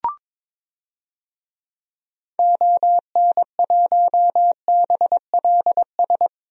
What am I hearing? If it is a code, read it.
Morse code, heard as a keyed tone, about 22 wpm, 700 Hz. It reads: ON1BLS
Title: DXCC-100 Call Signs - 6 Characters - 22wpm